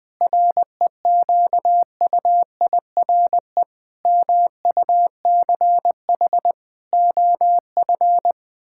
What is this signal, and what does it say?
Morse code, heard as a keyed tone, about 20 wpm, 700 Hz. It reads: REQUIRE MUCH OF
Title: Sentences from Top 1000 Words - Review Part 6 - 20wpm